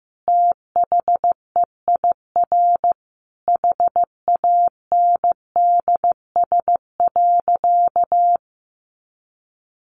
Morse code, heard as a keyed tone, about 15 wpm, 700 Hz. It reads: THEIR HANDS.
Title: Easy Phrases 15wpm